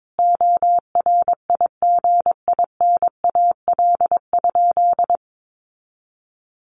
Morse code, heard as a keyed tone, about 22 wpm, 700 Hz. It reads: ORIGINAL?
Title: Sentences from Top 900 Words 22wpm